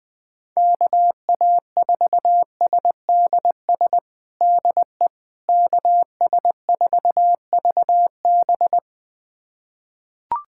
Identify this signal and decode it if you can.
Morse code, heard as a keyed tone, about 20 wpm, 700 Hz. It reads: KA4SDS DE KS4VB
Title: USA Callsign DE Callsign 20wpm